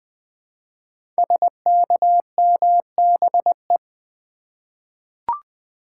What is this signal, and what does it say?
Morse code, heard as a keyed tone, about 21 wpm, 700 Hz. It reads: SKMBE
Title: Code Groups - 5 Letters - 20wpm